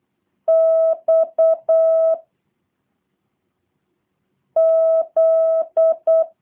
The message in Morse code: X Z

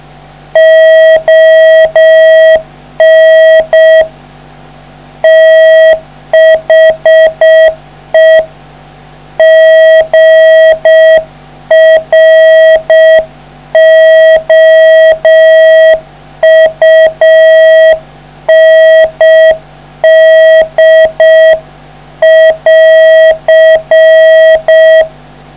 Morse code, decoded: ON THE GROUND<AR>

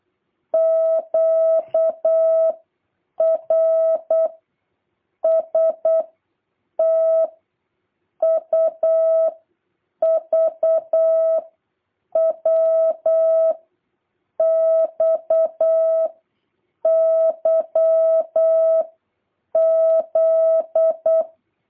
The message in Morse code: Q R S T U V W X Y Z

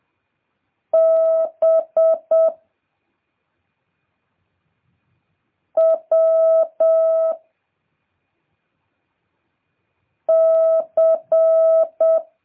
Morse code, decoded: B W C

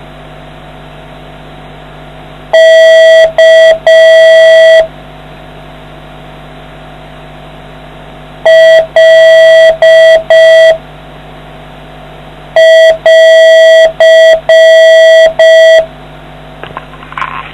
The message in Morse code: K L<AR>

